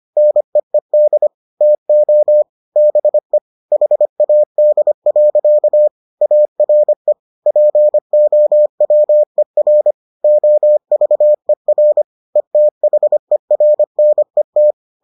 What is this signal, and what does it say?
Morse code, heard as a keyed tone, about 25 wpm, 600 Hz. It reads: NEED TO BE HAD. ARE POWER OVER ETHERNET